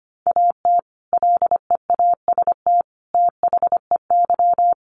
Morse code, heard as a keyed tone, about 25 wpm, 700 Hz. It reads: AT LEAST THEY
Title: Sentences from Top 1500 Words 25wpm